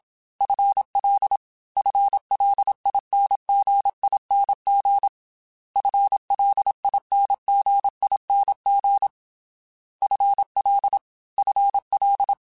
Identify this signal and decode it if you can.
Morse code, heard as a keyed tone, about 27 wpm, 800 Hz. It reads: FL FLINGING FLINGING FL FL
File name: FL / FL mp3s / FL Prefixes 3x2 Twice @25